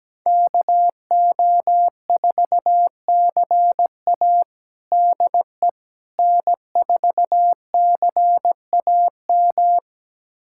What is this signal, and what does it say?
Morse code, heard as a keyed tone, about 17 wpm, 700 Hz. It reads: KO4CA DE N4CAM
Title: USA Callsign DE Callsign - 5 Characters - 17wpm